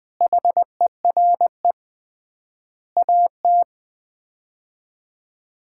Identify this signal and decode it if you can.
Morse code, heard as a keyed tone, about 21 wpm, 700 Hz. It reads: HERE AT
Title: 3x Word-Spacing - Sets of 5 Words - 20wpm